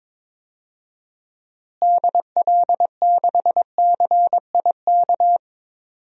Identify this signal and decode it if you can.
Morse code, heard as a keyed tone, about 21 wpm, 700 Hz. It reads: DL6CIK
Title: DXCC-100 Call Signs - 6 Characters - Repeated 3x - 22wpm